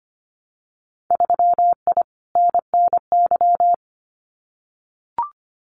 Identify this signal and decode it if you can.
Morse code, heard as a keyed tone, about 25 wpm, 700 Hz. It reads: 3I NNY
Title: Winter Field Day Exchanges 25wpm